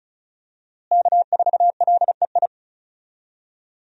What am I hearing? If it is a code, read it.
Morse code, heard as a keyed tone, about 36 wpm, 700 Hz. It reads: K4LEI
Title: USA Call Signs - 5 Characters - Repeated 3x - 35wpm